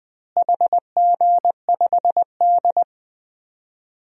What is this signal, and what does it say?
Morse code, heard as a keyed tone, about 21 wpm, 700 Hz. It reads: HG5D
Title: DXCC-100 Call Signs - 4 Characters - Repeated 3x - 20wpm